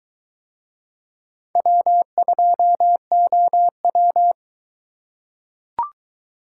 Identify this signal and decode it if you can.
Morse code, heard as a keyed tone, about 21 wpm, 700 Hz. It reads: W2OW